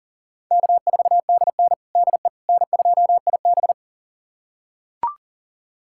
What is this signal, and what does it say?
Morse code, heard as a keyed tone, about 40 wpm, 700 Hz. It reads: K4DN DE N2IB
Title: USA Callsign DE Callsign - 4 Characters - 40wpm